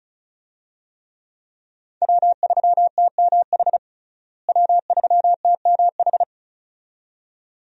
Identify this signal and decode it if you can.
Morse code, heard as a keyed tone, about 36 wpm, 700 Hz. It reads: W3TMH W3TMH
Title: USA Call Signs - 5 Characters - Repeated 2x - 35wpm